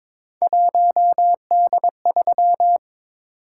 Morse code, heard as a keyed tone, about 22 wpm, 700 Hz. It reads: 1D3
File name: practice / 22wpm / Three Letter-Number Combinations - 22wpm